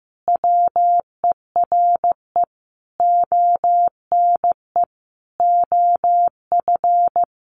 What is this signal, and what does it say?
Morse code, heard as a keyed tone, about 15 wpm, 700 Hz. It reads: WERE ONE OF
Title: Sentences from Top 1500 Words 15wpm